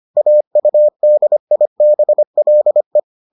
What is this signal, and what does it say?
Morse code, heard as a keyed tone, about 25 wpm, 600 Hz. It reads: AUDIBLE